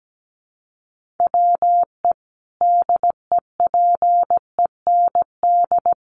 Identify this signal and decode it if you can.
Morse code, heard as a keyed tone, about 17 wpm, 700 Hz. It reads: WE DEPEND
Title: Sentences from Top 1300 Words 17wpm